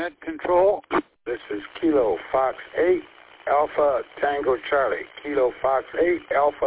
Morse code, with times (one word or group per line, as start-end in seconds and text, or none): none